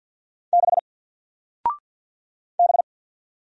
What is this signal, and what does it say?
Morse code, heard as a keyed tone, about 48 wpm, 700 Hz. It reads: 6 B
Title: Single Character (Letter:Number:Punctuation:Prosign) 50wpm